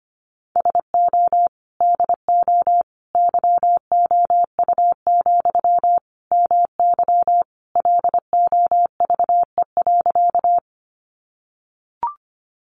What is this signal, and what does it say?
Morse code, heard as a keyed tone, about 25 wpm, 700 Hz. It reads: SO DO YOU, MY LOVE.
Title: Sentences from Top 400 Words 25wpm